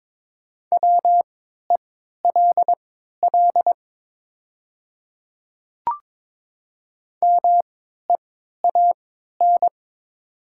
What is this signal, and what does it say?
Morse code, heard as a keyed tone, about 22 wpm, spaced as for 13 wpm, 700 Hz. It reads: WELL MEAN
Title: 3x Character Spacing - Top 500 Words - 22wpm